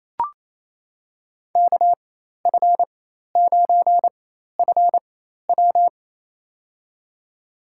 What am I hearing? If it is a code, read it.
Morse code, heard as a keyed tone, about 28 wpm, spaced as for 13 wpm, 700 Hz. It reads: KF9FW
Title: USA Call Signs - 5 Characters - 4x Character Spacing - 28wpm